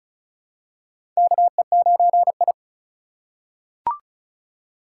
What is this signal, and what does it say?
Morse code, heard as a keyed tone, about 36 wpm, 700 Hz. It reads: KE9I
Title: K1USN SST QSO Practice - 35wpm